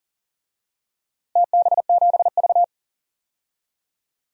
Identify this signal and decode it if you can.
Morse code, heard as a keyed tone, about 39 wpm, 700 Hz. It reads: TB7V
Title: DXCC-100 Call Signs - 4 Characters - Repeated 3x - 40wpm